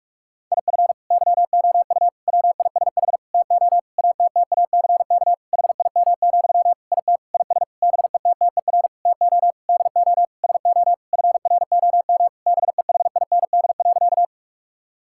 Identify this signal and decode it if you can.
Morse code, heard as a keyed tone, about 46 wpm, 700 Hz. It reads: IF YOU WISH TO ATTACK HIM, IT IS BETTER TO DO SO FROM BEHIND.
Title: Sentences from Top 1600 Words 45wpm